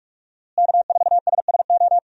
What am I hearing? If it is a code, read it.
Morse code, heard as a keyed tone, about 46 wpm, 700 Hz. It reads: K4SSO
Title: USA Call Signs - 5 Characters - Repeated 2x - 45wpm